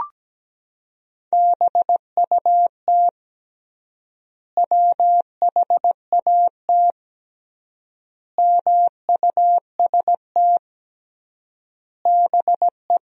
Morse code, heard as a keyed tone, about 17 wpm, 700 Hz. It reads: BUT WHAT MUST BE
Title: 3x Word-Spacing - Sentences from Top 500 Words - 17wpm